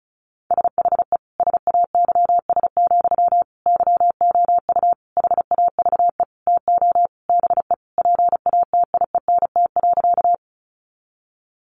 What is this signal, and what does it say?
Morse code, heard as a keyed tone, about 34 wpm, 700 Hz. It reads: SHE SAYS, YOU HAVE TO BE PATIENT.
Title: Easy Phrases 35wpm